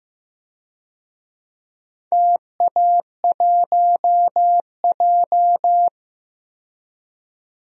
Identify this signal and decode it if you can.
Morse code, heard as a keyed tone, about 15 wpm, 700 Hz. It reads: TA1J
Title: DXCC-100 Call Signs - 4 Characters - 15wpm